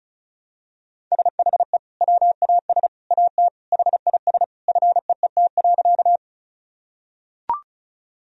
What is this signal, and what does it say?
Morse code, heard as a keyed tone, about 36 wpm, 700 Hz. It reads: SHE WAS AT HIS FEET.
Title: Sentences from Top 1000 Words - Review Part 1 - 35wpm